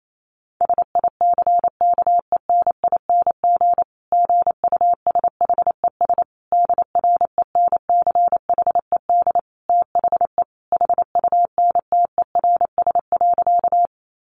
Morse code, heard as a keyed tone, about 28 wpm, 700 Hz. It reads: SICKENING GUSHES DRENCHED THE HUNTERS.
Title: A Sound Of Thunder 28wpm